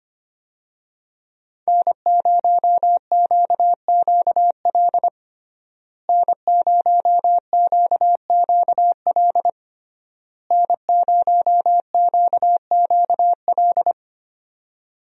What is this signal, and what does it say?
Morse code, heard as a keyed tone, about 25 wpm, 700 Hz. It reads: N0QQL N0QQL N0QQL
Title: USA Call Signs - 5 Characters - Repeated 3x - 25wpm